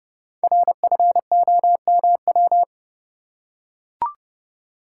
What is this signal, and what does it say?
Morse code, heard as a keyed tone, about 30 wpm, 700 Hz. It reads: RFOMW
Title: Code Groups - 5 Letters - 30wpm